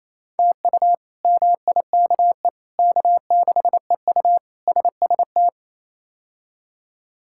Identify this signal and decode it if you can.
Morse code, heard as a keyed tone, about 28 wpm, 700 Hz. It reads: TU MIKE K6EU SST
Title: K1USN SST QSO Practice - 28wpm